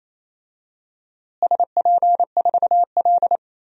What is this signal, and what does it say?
Morse code, heard as a keyed tone, about 28 wpm, 700 Hz. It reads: SP4L
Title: DXCC-100 Call Signs - 4 Characters - Repeated 3x - 28wpm